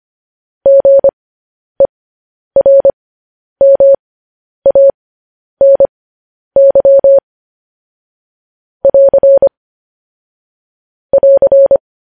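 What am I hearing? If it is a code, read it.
Morse code, heard as a keyed tone, about 25 wpm, spaced as for 10 wpm, 550 Hz. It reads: GERMANY <AR> <AR>
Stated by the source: Morse transmission of news headlines.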